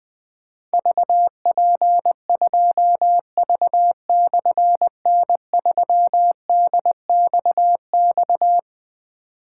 Morse code, heard as a keyed tone, about 20 wpm, 700 Hz. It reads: VP2V/N3DXX
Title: Difficult Call Signs 20wpm